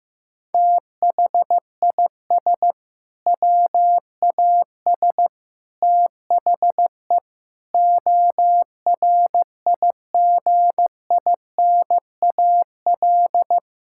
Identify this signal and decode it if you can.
Morse code, heard as a keyed tone, about 15 wpm, 700 Hz. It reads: THIS WAS THE ORIGINAL
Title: Sentences from Top 1000 Words - Review Part 6 - 15wpm